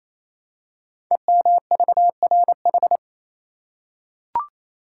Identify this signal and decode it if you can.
Morse code, heard as a keyed tone, about 27 wpm, 700 Hz. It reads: EMVRH
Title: Code Groups - 5 Letters - 28wpm